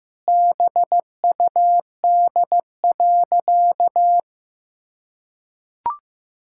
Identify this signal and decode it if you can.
Morse code, heard as a keyed tone, about 15 wpm, 700 Hz. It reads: BUD.
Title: Pangram Challenge v2 - 15wpm